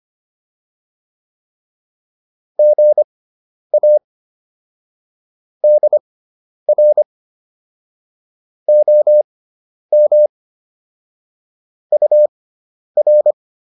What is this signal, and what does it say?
Morse code, heard as a keyed tone, about 25 wpm, spaced as for 10 wpm, 600 Hz. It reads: GA DR OM UR